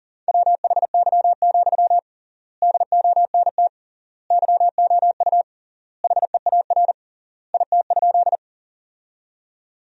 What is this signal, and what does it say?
Morse code, heard as a keyed tone, about 40 wpm, 700 Hz. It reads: WHY, DONT YOU HEAR IT?
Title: 3x Word-Spacing - Sentences from Top 300 Words - 40wpm